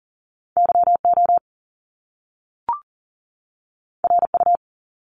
Morse code, heard as a keyed tone, about 38 wpm, 700 Hz. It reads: YO RU
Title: Top Two Letter Combinations in Words 40wpm